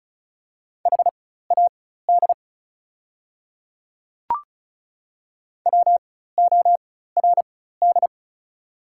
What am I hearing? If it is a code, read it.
Morse code, heard as a keyed tone, about 36 wpm, spaced as for 16 wpm, 700 Hz. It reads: HAD WORD
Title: 4x Character Spacing - Top 100 Words - 35wpm